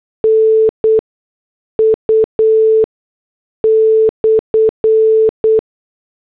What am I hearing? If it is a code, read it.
Morse code, heard as a keyed tone, about 8 wpm, spaced as for 6 wpm, 440 Hz. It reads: NU/